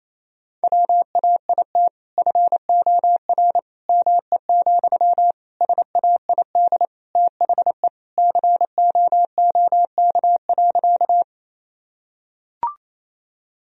Morse code, heard as a keyed tone, about 28 wpm, 700 Hz. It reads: WAIT FOR ME, SAID THE COOK.
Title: Sentences from Top 1000 Words - Review Part 4 - 28wpm